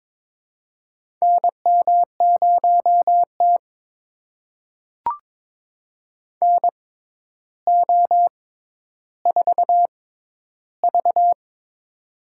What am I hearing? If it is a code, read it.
Morse code, heard as a keyed tone, about 22 wpm, spaced as for 11 wpm, 700 Hz. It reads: NM0T NO4V
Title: USA Call Signs - 4 Characters - 6x Character Spacing - 22wpm